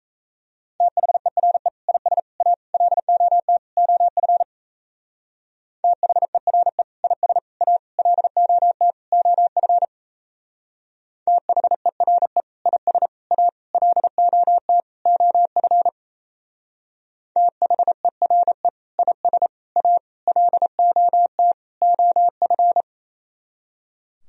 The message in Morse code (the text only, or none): THERE IS A LOT OF THERE IS A LOT OF THERE IS A LOT OF THERE IS A LOT OF